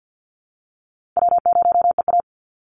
Morse code, heard as a keyed tone, about 50 wpm, 700 Hz. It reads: W0EA
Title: Easy Call Signs Repeated 3x - 50wpm